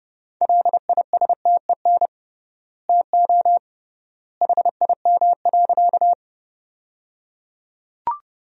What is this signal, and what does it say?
Morse code, heard as a keyed tone, about 28 wpm, 700 Hz. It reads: LISTEN TO HIM.
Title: 3x Word-Spacing - Sentences from Top 400 Words - 30wpm